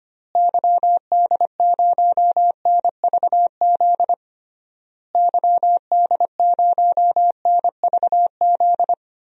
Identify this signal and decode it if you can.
Morse code, heard as a keyed tone, about 25 wpm, 700 Hz. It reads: YD0NVZ YD0NVZ
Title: DXCC-100 Call Signs - 6 Characters - Repeated 3x - 25wpm